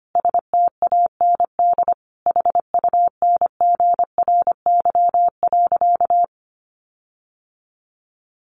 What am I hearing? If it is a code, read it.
Morse code, heard as a keyed tone, about 25 wpm, 700 Hz. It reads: STAND HUNGRY.
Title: Sentences from Top 1400 Words 25wpm